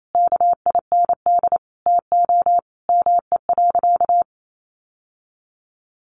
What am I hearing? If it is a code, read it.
Morse code, heard as a keyed tone, about 28 wpm, 700 Hz. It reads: KIND TO ME.
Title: Sentences from Top 1100 Words 28wpm